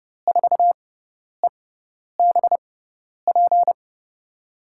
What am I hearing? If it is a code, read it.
Morse code, heard as a keyed tone, about 29 wpm, spaced as for 10 wpm, 700 Hz. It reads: 4EBP
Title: USA Call Signs - 6 Characters - 6x Character Spacing - 30wpm